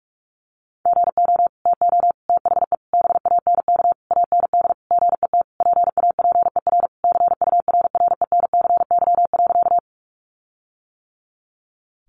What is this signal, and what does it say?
Morse code, heard as a keyed tone, about 44 wpm, 700 Hz. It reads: GO TO THE BANK AND GET PAPER CURRENCY.
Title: Sentences from Top 1500 Words 45wpm